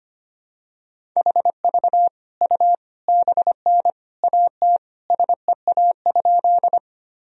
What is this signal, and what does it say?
Morse code, heard as a keyed tone, about 25 wpm, 700 Hz. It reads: HV U BN AT SEA?
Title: Sentences from Top 300 Words and Abbreviations 25wpm